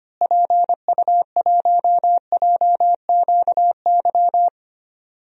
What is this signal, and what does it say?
Morse code, heard as a keyed tone, about 25 wpm, 700 Hz. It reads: PU1JQY
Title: DXCC-100 Call Signs - 6 Characters - Repeated 3x - 25wpm